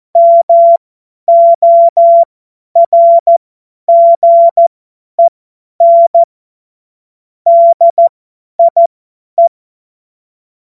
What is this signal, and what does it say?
Morse code, heard as a keyed tone, about 14 wpm, spaced as for 10 wpm, 680 Hz. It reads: MORGEN DIE